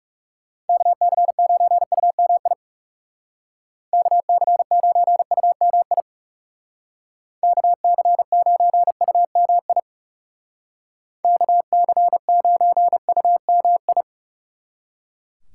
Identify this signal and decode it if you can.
Morse code, heard as a keyed tone, about 35 wpm, 700 Hz. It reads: KC9UMI KC9UMI KC9UMI KC9UMI
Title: Speed Racing - USA 2x3 Call Signs 30wpm